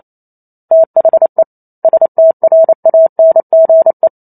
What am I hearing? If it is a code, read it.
Morse code, heard as a keyed tone, about 28 wpm, 650 Hz. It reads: THE STRANGE